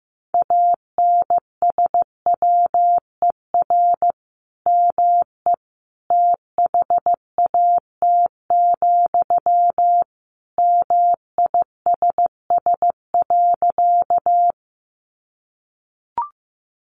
Morse code, heard as a keyed tone, about 15 wpm, 700 Hz. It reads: ANSWER ME THAT, MISS.
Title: Sentences from Top 500 Words 15wpm